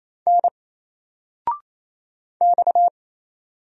Morse code, heard as a keyed tone, about 27 wpm, 700 Hz. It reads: N X
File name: Single-Letter-Number-28wpm